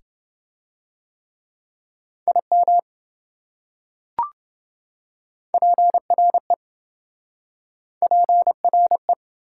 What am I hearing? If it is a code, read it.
Morse code, heard as a keyed tone, about 27 wpm, 700 Hz. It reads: IM PRE PRE